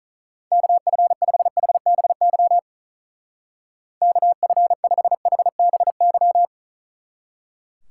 Morse code, heard as a keyed tone, about 45 wpm, 700 Hz. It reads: KF5HBY KF5HBY